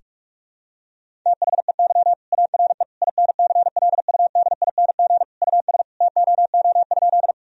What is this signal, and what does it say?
Morse code, heard as a keyed tone, about 46 wpm, 700 Hz. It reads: THEY ARE INCLUDING US TOO?
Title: Sentences from Top 1000 Words - Review Part 4 - 45wpm